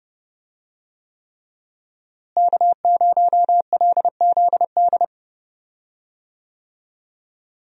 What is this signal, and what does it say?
Morse code, heard as a keyed tone, about 30 wpm, 700 Hz. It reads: K0LZD
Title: USA Call Signs - 5 Characters - Repeated 3x - 30wpm